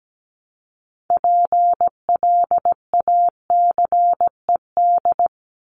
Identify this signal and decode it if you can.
Morse code, heard as a keyed tone, about 17 wpm, 700 Hz. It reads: PLACED
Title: ED Suffix - Top 100 Words with Base Words 17wpm